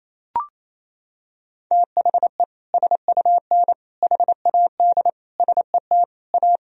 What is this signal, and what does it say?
Morse code, heard as a keyed tone, about 28 wpm, 700 Hz. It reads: THE SUN HAD SET A
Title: Sentences from Top 1000 Words - Review Part 1 - 28wpm